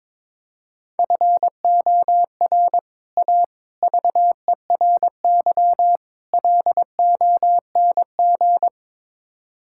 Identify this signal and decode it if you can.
Morse code, heard as a keyed tone, about 22 wpm, 700 Hz. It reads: FOR A VERY LONG
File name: practice / 22wpm / Build A Phrase - 5-Word Phrases - 22wpm